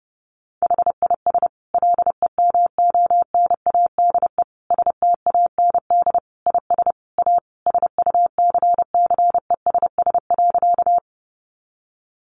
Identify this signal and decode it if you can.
Morse code, heard as a keyed tone, about 30 wpm, 700 Hz. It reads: HIS LEMONADE STAND IS A SUCCESS.
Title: Additional Easy Phrases 30wpm